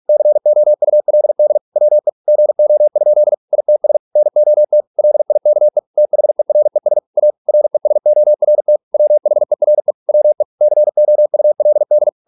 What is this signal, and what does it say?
Morse code, heard as a keyed tone, about 46 wpm, 600 Hz. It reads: COULD WE GO? ITS NOT LIKE THERES A RESORT WHERE WE COULD